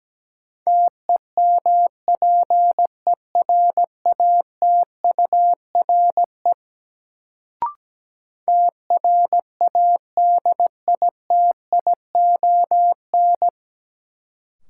Morse code, heard as a keyed tone, about 17 wpm, 700 Hz. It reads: TEMPERATURE TRADITION
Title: Sentences from Top 1500 Words - New Word Drill Alt - 17wpm